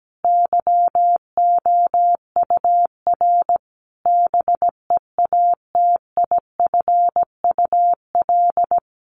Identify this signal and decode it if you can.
Morse code, heard as a keyed tone, about 17 wpm, 700 Hz. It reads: YOUR BEATIFUL
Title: Sentences from Top 1300 Words 17wpm